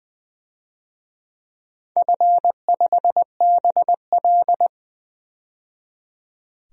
Morse code, heard as a keyed tone, about 20 wpm, 700 Hz. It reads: F5BL
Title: DXCC-100 Call Signs - 4 Characters - 20wpm